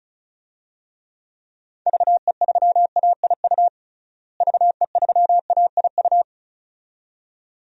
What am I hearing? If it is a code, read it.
Morse code, heard as a keyed tone, about 34 wpm, 700 Hz. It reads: VE3AIU VE3AIU
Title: DXCC-100 Call Signs - 6 Characters - Repeated 2x - 35wpm